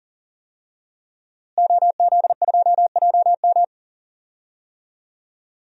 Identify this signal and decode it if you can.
Morse code, heard as a keyed tone, about 37 wpm, 700 Hz. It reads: OZ2JM
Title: DXCC-100 Call Signs - 5 Characters - Repeated 3x - 40wpm